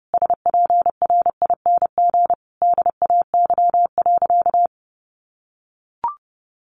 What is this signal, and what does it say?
Morse code, heard as a keyed tone, about 30 wpm, 700 Hz. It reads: SPRING DAY.
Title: Sentences from Top 1200 Words 30wpm